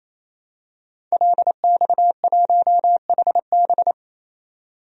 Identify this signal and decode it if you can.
Morse code, heard as a keyed tone, about 27 wpm, 700 Hz. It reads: LX1HB
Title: DXCC-100 Call Signs - 5 Characters - 28wpm